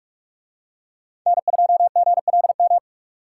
Morse code, heard as a keyed tone, about 49 wpm, 700 Hz. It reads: N2GLM